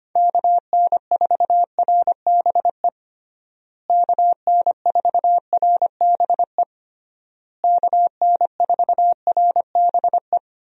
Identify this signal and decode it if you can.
Morse code, heard as a keyed tone, about 25 wpm, 700 Hz. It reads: KN4RBE KN4RBE KN4RBE
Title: USA Call Signs - 6 Characters - Repeated 3x - 25wpm